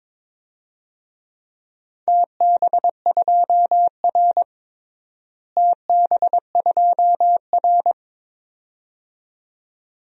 Morse code, heard as a keyed tone, about 21 wpm, 700 Hz. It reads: TB2R TB2R
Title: DXCC-100 Call Signs - 4 Characters - Repeated 2x - 22wpm